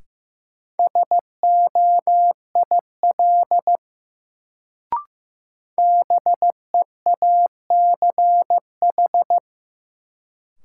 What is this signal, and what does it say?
Morse code, heard as a keyed tone, about 15 wpm, 700 Hz. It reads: SOIL BEACH
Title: Sentences from Top 1500 Words - New Word Drill Alt - 15wpm